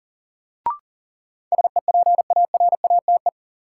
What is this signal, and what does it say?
Morse code, heard as a keyed tone, about 40 wpm, 700 Hz. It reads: SEPARATE
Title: Sentences from Top 1000 Words - Review Part 7 - New Word Drill - 40wpm